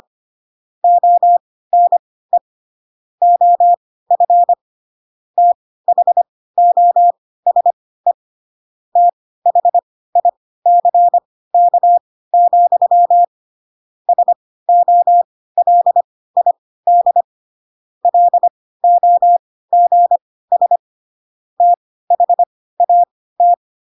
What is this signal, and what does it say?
Morse code, heard as a keyed tone, about 25 wpm, spaced as for 16 wpm, 700 Hz. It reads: ONE OF THOSE THICK, SOLID LOGS THAT
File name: Pinocchio / Pinocchio ch1 10 25 ik1hge